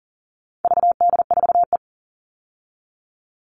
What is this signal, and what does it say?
Morse code, heard as a keyed tone, about 37 wpm, 700 Hz. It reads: VD4E
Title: All Cut Numbers - 4-Digit 40wpm